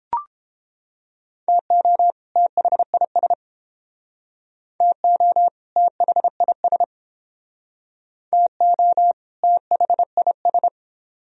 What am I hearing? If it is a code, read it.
Morse code, heard as a keyed tone, about 31 wpm, 700 Hz. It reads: TO THIS TO THIS TO THIS